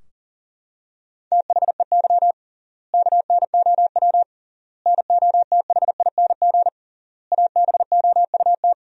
Morse code, heard as a keyed tone, about 40 wpm, 700 Hz. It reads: THEY KNOW NOTHING ABOUT
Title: 3x Word-Spacing - Sentences from Top 500 Words - 40wpm